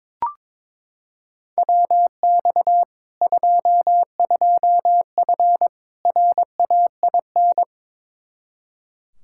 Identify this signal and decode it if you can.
Morse code, heard as a keyed tone, about 22 wpm, 700 Hz. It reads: WX 22F RAIN
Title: Practice QSO - Exchange 2 - Info - 22wpm